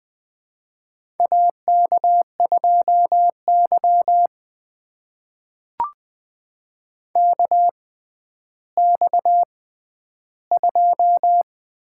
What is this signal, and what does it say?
Morse code, heard as a keyed tone, about 20 wpm, spaced as for 11 wpm, 700 Hz. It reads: AK2Y KX2
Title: USA Call Signs - 4 Characters - 6x Character Spacing - 20wpm